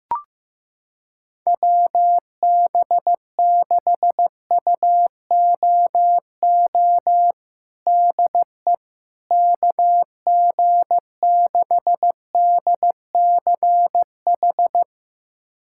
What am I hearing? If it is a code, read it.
Morse code, heard as a keyed tone, about 15 wpm, 700 Hz. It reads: WB6UOO DE KG6DCH
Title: USA Callsign DE Callsign 15wpm